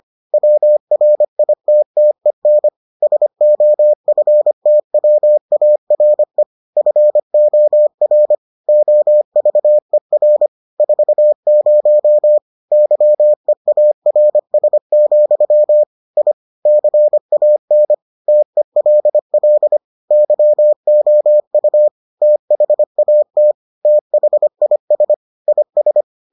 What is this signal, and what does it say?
Morse code, heard as a keyed tone, about 25 wpm, 600 Hz. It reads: WRITTEN SOFTWARE FOR OVER 40 YEARS, I CAN TELL YOU THAT THIS IS